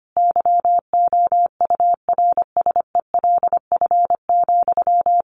Morse code, heard as a keyed tone, about 25 wpm, 700 Hz. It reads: YOURSELF,
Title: Sentences from Top 1300 Words 25wpm